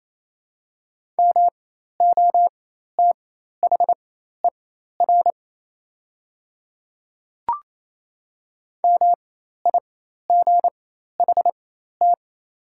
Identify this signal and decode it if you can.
Morse code, heard as a keyed tone, about 28 wpm, spaced as for 13 wpm, 700 Hz. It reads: MOTHER MIGHT
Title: 4x Character Spacing - Top 300 Words - 28wpm